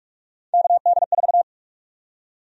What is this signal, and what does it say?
Morse code, heard as a keyed tone, about 46 wpm, 700 Hz. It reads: KD4
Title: USA Call Sign Prefixes - 45wpm